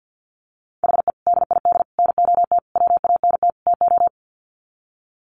Morse code, heard as a keyed tone, about 50 wpm, 700 Hz. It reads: HE DID NOT WANT TO
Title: Sets of 5 Words Encore 50wpm